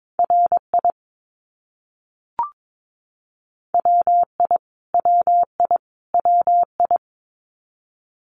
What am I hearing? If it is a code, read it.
Morse code, heard as a keyed tone, about 22 wpm, 700 Hz. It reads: RI WI WI WI
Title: US States Repeated 3x 22wpm